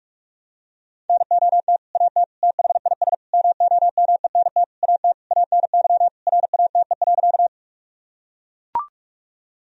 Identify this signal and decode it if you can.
Morse code, heard as a keyed tone, about 41 wpm, 700 Hz. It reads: NOT AT THIS MOMENT AT ANY RATE.
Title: Sentences from Top 1000 Words - Review Part 4 - 45wpm